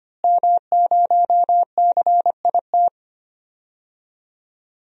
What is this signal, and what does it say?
Morse code, heard as a keyed tone, about 25 wpm, 700 Hz. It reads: M0CIT
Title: DXCC-100 Call Signs - 5 Characters - Repeated 3x - 25wpm